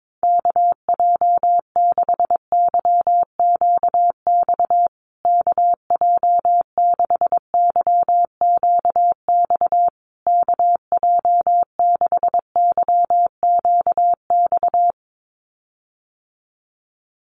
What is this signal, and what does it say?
Morse code, heard as a keyed tone, about 22 wpm, 700 Hz. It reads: KJ6YQX KJ6YQX KJ6YQX
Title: USA Call Signs Repeated 3x 22wpm